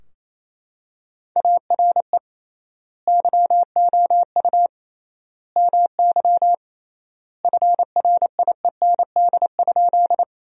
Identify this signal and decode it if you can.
Morse code, heard as a keyed tone, about 27 wpm, 700 Hz. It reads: ARE YOU MY FRIEND?
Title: 3x Word-Spacing - Sentences from Top 300 Words - 28wpm